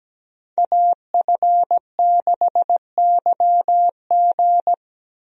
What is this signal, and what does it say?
Morse code, heard as a keyed tone, about 17 wpm, 700 Hz. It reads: AF6YG
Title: USA Call Signs - 5 Characters - 17wpm